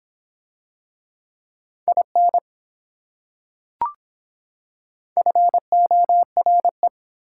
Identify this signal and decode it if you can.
Morse code, heard as a keyed tone, about 24 wpm, 700 Hz. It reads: IN FORE